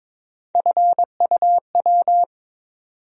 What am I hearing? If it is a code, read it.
Morse code, heard as a keyed tone, about 22 wpm, 700 Hz. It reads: FUW
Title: USA Call Sign Postfixes - 3 Letters 22wpm